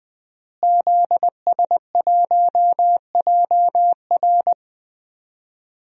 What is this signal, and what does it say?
Morse code, heard as a keyed tone, about 20 wpm, 700 Hz. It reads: ZS1JR
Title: DXCC-100 Call Signs - 5 Characters - Repeated 3x - 20wpm